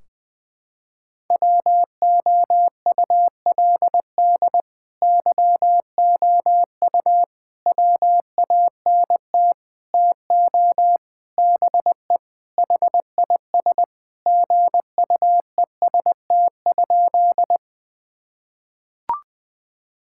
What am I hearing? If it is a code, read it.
Morse code, heard as a keyed tone, about 20 wpm, 700 Hz. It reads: WOULD YOU WANT TO BE HIS GUEST?
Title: Sentences from Top 1600 Words 20wpm